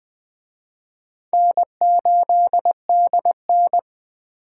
Morse code, heard as a keyed tone, about 21 wpm, 700 Hz. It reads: N8DN